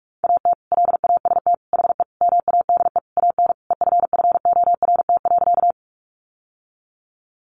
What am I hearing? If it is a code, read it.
Morse code, heard as a keyed tone, about 44 wpm, 700 Hz. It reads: AT LAST HE MADE AN EFFORT.
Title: Sentences from Top 1200 Words 45wpm